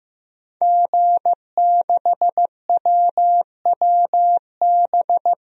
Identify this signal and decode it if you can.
Morse code, heard as a keyed tone, about 15 wpm, 700 Hz. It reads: G6WWB
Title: DXCC-100 Call Signs - 5 Characters - Repeated 2x - 15wpm